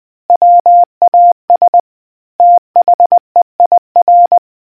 Morse code, heard as a keyed tone, about 20 wpm, 700 Hz. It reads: WAS THEIR